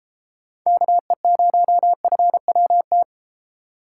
Morse code, heard as a keyed tone, about 31 wpm, 700 Hz. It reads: KE0FWT